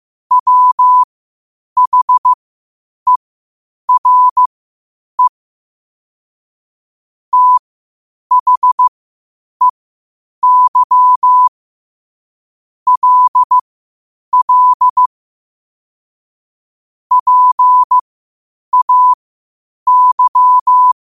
Morse code, heard as a keyed tone, about 15 wpm, spaced as for 8 wpm, 1000 Hz. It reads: WHERE THEY LL PAY
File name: qotdfiles / 8wpm